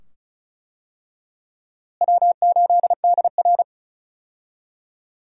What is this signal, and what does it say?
Morse code, heard as a keyed tone, about 36 wpm, 700 Hz. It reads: W8DR
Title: USA Call Signs - 4 Characters - Repeated 2x - 35wpm